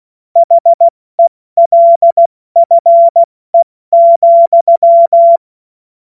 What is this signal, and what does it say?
Morse code, heard as a keyed tone, about 16 wpm, 680 Hz. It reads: HELFE,
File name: Morsekurs / Witze / Witz 20. Januar 2026 70 F80